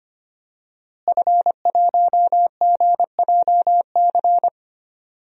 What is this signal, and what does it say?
Morse code, heard as a keyed tone, about 25 wpm, 700 Hz. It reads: F1GJC
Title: DXCC-100 Call Signs - 5 Characters - Repeated 3x - 25wpm